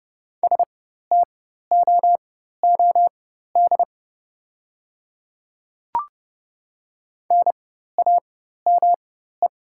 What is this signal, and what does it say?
Morse code, heard as a keyed tone, about 30 wpm, spaced as for 14 wpm, 690 Hz. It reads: STOOD NAME
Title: 4x Character Spacing - Top 500 Words - 30wpm